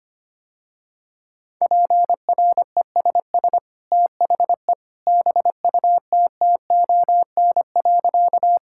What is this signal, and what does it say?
Morse code, heard as a keyed tone, about 25 wpm, 700 Hz. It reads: PRESS THE BUTTON.
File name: Easy Sentences with Hints - 25wpm